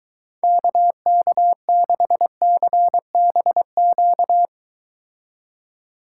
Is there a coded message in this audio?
Yes: Morse code from 0.4 to 4.5 s, heard as a keyed tone, about 22 wpm, 700 Hz. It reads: KK6CBQ